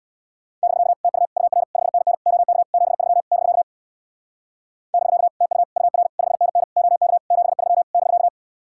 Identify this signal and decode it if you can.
Morse code, heard as a keyed tone, about 67 wpm, 700 Hz. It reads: <BK> TU UR 5NN OK 73 <BK> <BK> TU UR 5NN OK 73 <BK>